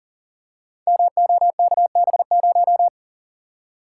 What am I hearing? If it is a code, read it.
Morse code, heard as a keyed tone, about 40 wpm, 700 Hz. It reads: MOKB0
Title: Code Groups - 5 Letter:Numbers - 40wpm